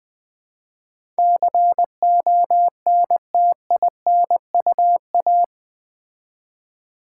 Morse code, heard as a keyed tone, about 20 wpm, 700 Hz. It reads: CONTINUA
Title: Build A Word - 4 Letters At A Time - 11-Letter Words - 20wpm